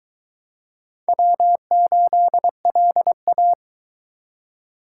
Morse code, heard as a keyed tone, about 21 wpm, 700 Hz. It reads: W8LA